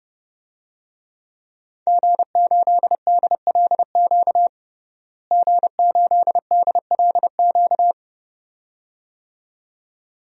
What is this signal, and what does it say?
Morse code, heard as a keyed tone, about 32 wpm, 700 Hz. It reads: G8DLQ G8DLQ
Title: DXCC-100 Call Signs - 5 Characters - Repeated 2x - 30wpm